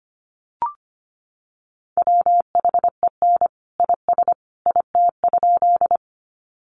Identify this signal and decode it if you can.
Morse code, heard as a keyed tone, about 25 wpm, 700 Hz. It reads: WHEN IS IT?
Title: Sentences from Top 100 Words 25wpm